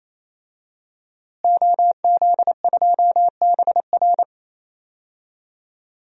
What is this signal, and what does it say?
Morse code, heard as a keyed tone, about 28 wpm, 700 Hz. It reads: OZ2BR
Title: DXCC-100 Call Signs - 5 Characters - Repeated 3x - 28wpm